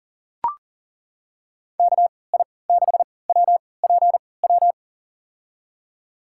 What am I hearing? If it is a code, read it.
Morse code, heard as a keyed tone, about 40 wpm, spaced as for 23 wpm, 700 Hz. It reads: KI6WPW
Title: USA Call Signs - 3x Character Spacing - 40wpm